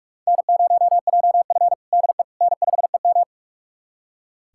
Morse code, heard as a keyed tone, about 42 wpm, 700 Hz. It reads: N0JF DE N5EM
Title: USA Callsign DE Callsign - 4 Characters - 45wpm